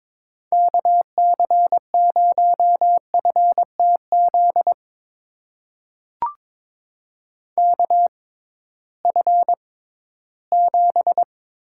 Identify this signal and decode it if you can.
Morse code, heard as a keyed tone, about 22 wpm, spaced as for 14 wpm, 700 Hz. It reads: KC0FTZ KF7
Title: USA Call Signs - 6x Character Spacing - 22wpm